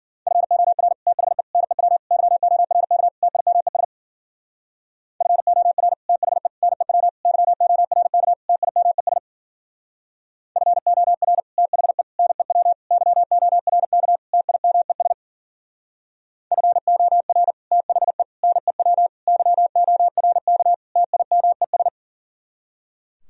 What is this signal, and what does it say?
Morse code, heard as a keyed tone, about 48 wpm, 700 Hz. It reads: FOR THE NEW YORK TIMES FOR THE NEW YORK TIMES FOR THE NEW YORK TIMES FOR THE NEW YORK TIMES
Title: Speed-Racing - Sets of 5 Words 40wpm